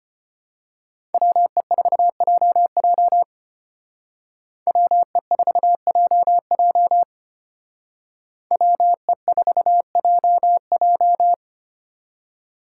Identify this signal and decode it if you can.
Morse code, heard as a keyed tone, about 29 wpm, 700 Hz. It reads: WE4JJ WE4JJ WE4JJ
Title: Speed Racing - USA 2x2 Call Signs 25wpm